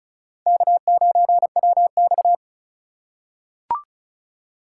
Morse code, heard as a keyed tone, about 36 wpm, 700 Hz. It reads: K9WX
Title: USA Call Signs - 4 Characters - 4x Character Spacing - 35wpm